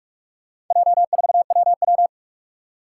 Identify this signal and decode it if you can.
Morse code, heard as a keyed tone, about 46 wpm, 700 Hz. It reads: J4WW
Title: Four Challenging Letter-Number Combinations 45wpm